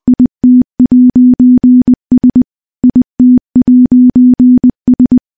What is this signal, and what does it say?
Morse code, heard as a keyed tone, about 20 wpm, 250 Hz. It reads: IT'S IT'S
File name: morse-training-20WPM